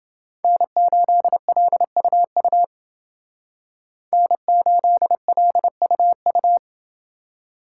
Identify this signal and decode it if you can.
Morse code, heard as a keyed tone, about 29 wpm, 700 Hz. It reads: N8LUU N8LUU